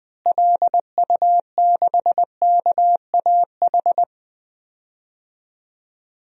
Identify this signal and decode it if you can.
Morse code, heard as a keyed tone, about 20 wpm, 700 Hz. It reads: LU6KAH
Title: DXCC-100 Call Signs - 6 Characters - Repeated 2x - 20wpm